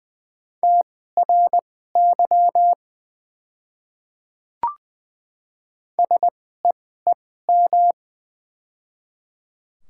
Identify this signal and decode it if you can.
Morse code, heard as a keyed tone, about 20 wpm, spaced as for 14 wpm, 700 Hz. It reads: TRY SEEM
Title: 2x Character Spacing - Top 300 Words - 20wpm